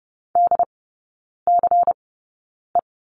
Morse code, heard as a keyed tone, about 30 wpm, spaced as for 9 wpm, 700 Hz. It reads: DCE
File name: Lesson 17e - Letter u - Three Characters 30wpm